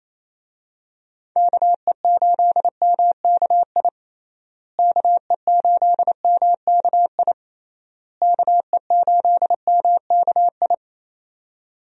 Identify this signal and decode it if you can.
Morse code, heard as a keyed tone, about 28 wpm, 700 Hz. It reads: KE8MKI KE8MKI KE8MKI
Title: USA Call Signs - 6 Characters - Repeated 3x - 28wpm